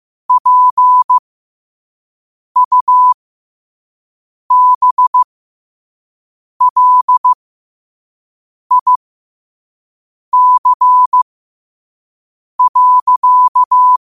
Morse code, heard as a keyed tone, about 15 wpm, spaced as for 5 wpm, 1000 Hz. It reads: PUBLIC.
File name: qotdfiles / 5wpm